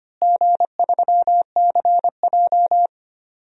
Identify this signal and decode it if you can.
Morse code, heard as a keyed tone, about 25 wpm, 700 Hz. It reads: G3CJ
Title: DXCC-100 Call Signs - 4 Characters - Repeated 3x - 25wpm